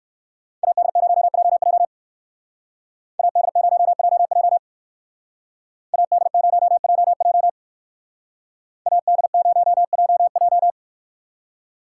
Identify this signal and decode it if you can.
Morse code, heard as a keyed tone, about 58 wpm, 700 Hz. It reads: AD0JJ AD0JJ AD0JJ AD0JJ